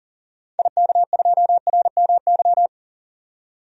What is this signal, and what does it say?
Morse code, heard as a keyed tone, about 38 wpm, 700 Hz. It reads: IK2RMY
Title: DXCC-100 Call Signs - 6 Characters - 40wpm